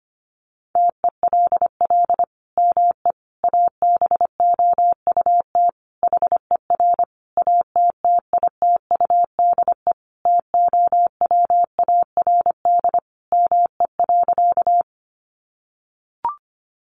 Morse code, heard as a keyed tone, about 25 wpm, 700 Hz. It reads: TELL ME ABOUT HER ATTITUDE TOWARD ME.
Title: Sentences from Top 1300 Words 25wpm